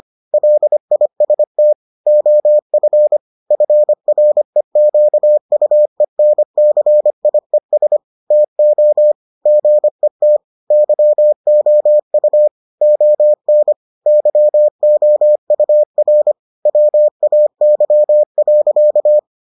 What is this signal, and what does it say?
Morse code, heard as a keyed tone, about 25 wpm, 600 Hz. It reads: LIST OF FREQUENCIES TO GET YOU ON YOUR WAY.